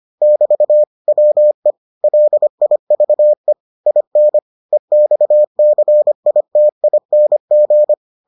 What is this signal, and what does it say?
Morse code, heard as a keyed tone, about 25 wpm, 600 Hz. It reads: <BT> WE LIVE IN EXCITING